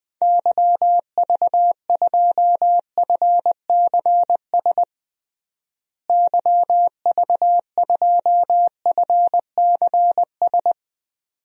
Morse code, heard as a keyed tone, about 21 wpm, 700 Hz. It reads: YV2FCS YV2FCS
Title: DXCC-100 Call Signs - 6 Characters - Repeated 2x - 20wpm